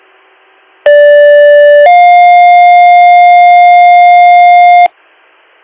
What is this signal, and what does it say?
Two-tone page, heard as a one-second tone, then a three-second tone, about 580 then 710 Hz. It is heard from 0.9 to 4.9 s.